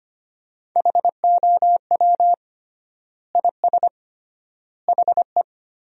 Morse code, heard as a keyed tone, about 25 wpm, 700 Hz. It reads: HOW IS HE
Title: 3x Word-Spacing - Sentences from Top 100 Words - 25wpm